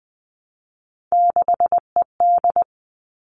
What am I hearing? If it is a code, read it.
Morse code, heard as a keyed tone, about 20 wpm, 700 Hz. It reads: 6ED
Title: All Cut Numbers - 3-Digit 20wpm